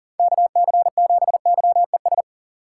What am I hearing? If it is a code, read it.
Morse code, heard as a keyed tone, about 39 wpm, 700 Hz. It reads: KC7YES
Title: USA Call Signs - 6 Characters - 4x Character Spacing - 40wpm